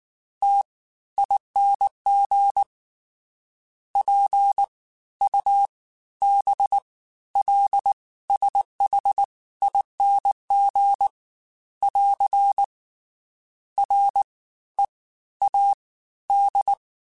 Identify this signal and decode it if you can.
Morse code, heard as a keyed tone, about 19 wpm, 800 Hz. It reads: T ING P U B L SH ING <AR> R E A D
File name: Special INGs-mp3s / Special INGs-mp3s / Fast ing @18